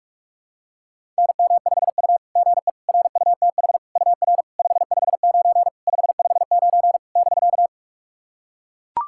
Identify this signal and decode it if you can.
Morse code, heard as a keyed tone, about 44 wpm, 690 Hz. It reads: NM5U GE RUTH UR 559 559 <BK>
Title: SOTA QSO Practice 45wpm